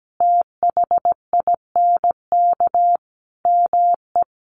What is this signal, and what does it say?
Morse code, heard as a keyed tone, about 17 wpm, 700 Hz. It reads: THINK ME
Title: Sentences from Top 1200 Words 17wpm